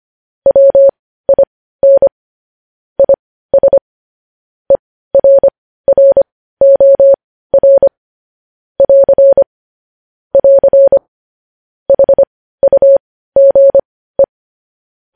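Morse code, heard as a keyed tone, about 25 wpm, spaced as for 15 wpm, 550 Hz. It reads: WIN IS ERROR <AR> <AR> HUGE
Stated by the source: News headlines sent in Morse code at 15 wpm.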